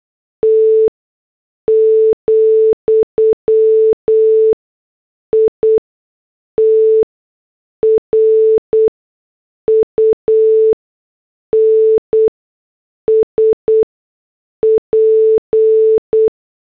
Morse code, heard as a keyed tone, about 8 wpm, spaced as for 6 wpm, 440 Hz. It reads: T, IT RUNS P